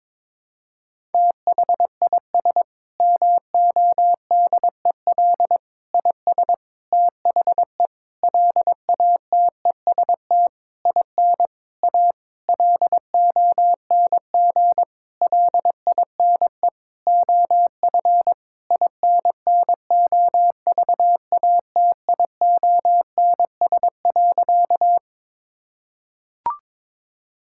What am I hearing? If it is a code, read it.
Morse code, heard as a keyed tone, about 21 wpm, 700 Hz. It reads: THIS MODEL IS THE LATEST IN A LONG LINE OF INNOVATIONS.
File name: practice / 22wpm / Sentences with 8-Word Phrases - 22wpm